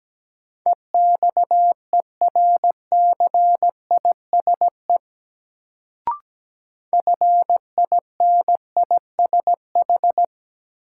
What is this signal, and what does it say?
Morse code, heard as a keyed tone, about 17 wpm, 700 Hz. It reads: EXERCISE FINISH
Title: Sentences from Top 1200 Words - New Word Drill Alt - 17wpm